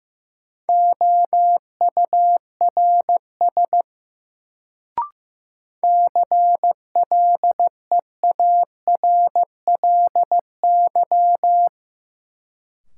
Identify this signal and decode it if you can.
Morse code, heard as a keyed tone, about 15 wpm, 700 Hz. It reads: OURS CLEARLY
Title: Sentences from Top 1300 Words - New Word Drill Alt - 15wpm